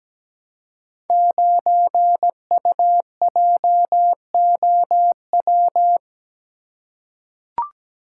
Morse code, heard as a keyed tone, about 17 wpm, 700 Hz. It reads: 9UJOW
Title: Code Groups - 5 Letter:Numbers - 17wpm